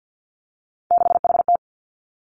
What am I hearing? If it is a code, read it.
Morse code, heard as a keyed tone, about 50 wpm, 700 Hz. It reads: 6HT
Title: Three Challenging Letter-Number Combinations 50wpm